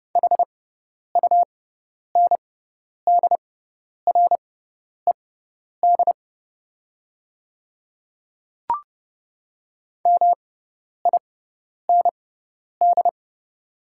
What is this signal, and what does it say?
Morse code, heard as a keyed tone, about 30 wpm, spaced as for 10 wpm, 690 Hz. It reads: HUNDRED MIND
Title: 6x Character Spacing - Top 500 Words - 30wpm